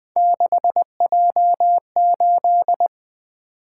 Morse code, heard as a keyed tone, about 21 wpm, 700 Hz. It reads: 6J8
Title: Three Letter-Number Combinations - 20wpm